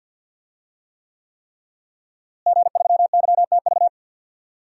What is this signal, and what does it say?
Morse code, heard as a keyed tone, about 52 wpm, 700 Hz. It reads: G3YTV
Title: DXCC-100 Call Signs - 5 Characters - Repeated 3x - 50wpm